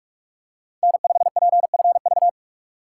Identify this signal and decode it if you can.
Morse code, heard as a keyed tone, about 42 wpm, 700 Hz. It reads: N5PFV
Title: USA Call Signs - 5 Characters - 4x Character Spacing - 45wpm